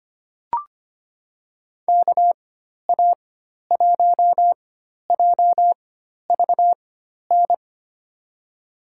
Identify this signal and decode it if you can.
Morse code, heard as a keyed tone, about 25 wpm, spaced as for 12 wpm, 700 Hz. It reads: KA1JVN
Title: USA Call Signs - 6 Characters - 4x Character Spacing - 25wpm